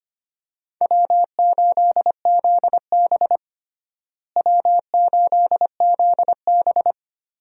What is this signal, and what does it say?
Morse code, heard as a keyed tone, about 25 wpm, 700 Hz. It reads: W8ZB W8ZB
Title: USA Call Signs - 4 Characters - Repeated 3x - 25wpm